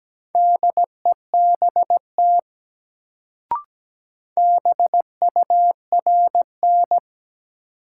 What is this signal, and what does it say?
Morse code, heard as a keyed tone, about 17 wpm, 700 Hz. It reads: DEBT BURN
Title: Sentences from Top 1300 Words - New Word Drill Alt - 17wpm